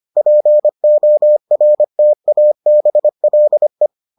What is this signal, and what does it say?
Morse code, heard as a keyed tone, about 25 wpm, 600 Hz. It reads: PORTABLE